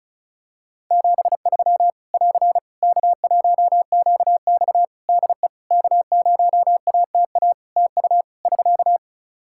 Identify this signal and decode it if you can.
Morse code, heard as a keyed tone, about 36 wpm, 700 Hz. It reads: 73 <AR> K1QX DE K0ATA TU <SK>
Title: Practice QSO - Exchange 4 - Ending - 35wpm